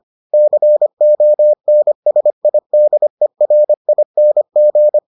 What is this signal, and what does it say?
Morse code, heard as a keyed tone, about 25 wpm, 600 Hz. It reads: CONSIDERING